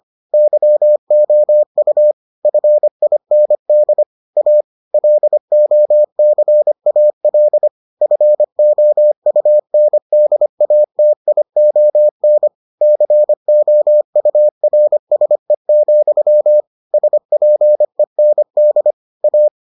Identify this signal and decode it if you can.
Morse code, heard as a keyed tone, about 25 wpm, 600 Hz. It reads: YOU FIND A LOCAL FOUNDATION COURSE, SPEND A